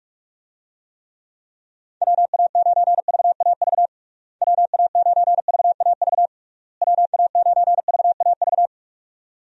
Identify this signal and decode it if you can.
Morse code, heard as a keyed tone, about 46 wpm, 700 Hz. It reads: WA9VAV WA9VAV WA9VAV
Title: USA Call Signs - 6 Characters - Repeated 3x - 45wpm